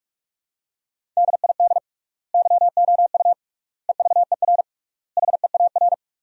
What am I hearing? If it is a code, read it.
Morse code, heard as a keyed tone, about 46 wpm, 700 Hz. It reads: DID YOU EVER HEAR
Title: 3x Word-Spacing - Sentences from Top 400 Words - 45wpm